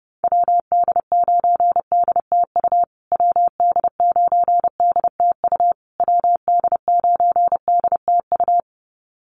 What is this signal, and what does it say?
Morse code, heard as a keyed tone, about 30 wpm, 700 Hz. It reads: WD9DTU WD9DTU WD9DTU
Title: USA Call Signs Repeated 3x 30wpm